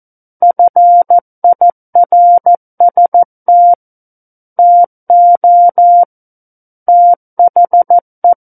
Morse code, heard as a keyed tone, about 14 wpm, 700 Hz. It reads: FIRST TO THE
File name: Book One / 05-Chapter 5 - 14 WPM